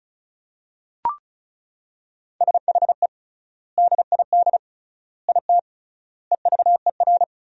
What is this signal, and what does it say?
Morse code, heard as a keyed tone, about 36 wpm, 700 Hz. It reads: SHE DID IT EVER
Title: 3x Word-Spacing - Sentences from Top 400 Words - 35wpm